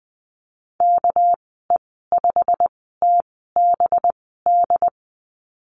Morse code, heard as a keyed tone, about 20 wpm, spaced as for 15 wpm, 700 Hz. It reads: KE5TBD
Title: 231 Lesson 37g - Letter x - Call Signs 20wpm